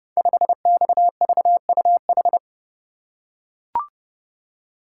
Morse code, heard as a keyed tone, about 28 wpm, 700 Hz. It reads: 5XVUH
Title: Code Groups - 5 Letter:Numbers - 30wpm